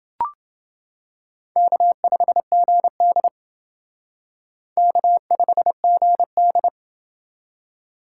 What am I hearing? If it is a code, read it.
Morse code, heard as a keyed tone, about 28 wpm, 700 Hz. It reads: K5GD K5GD